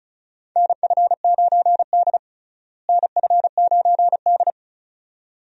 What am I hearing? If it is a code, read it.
Morse code, heard as a keyed tone, about 34 wpm, 700 Hz. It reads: NF9D NF9D
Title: USA Call Signs - 4 Characters - Repeated 2x - 35wpm